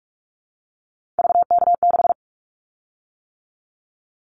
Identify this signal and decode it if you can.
Morse code, heard as a keyed tone, about 44 wpm, 700 Hz. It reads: VK6
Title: Callsign Prefixes for Top 100 DXCC - 45wpm